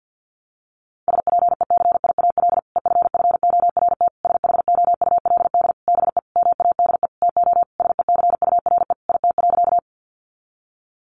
Morse code, heard as a keyed tone, about 49 wpm, 700 Hz. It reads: SPECIAL EFFORT SHOULD BE MADE TO SECURE IT.
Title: Sentences from Top 1200 Words 50wpm